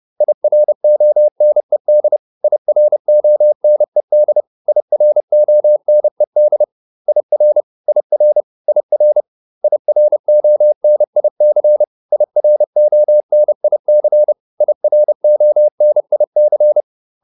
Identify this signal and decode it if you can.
Morse code, heard as a keyed tone, about 30 wpm, 600 Hz. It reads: IRONED IRONED IRONED IR IR IR IRONIC IRONIC IRONIC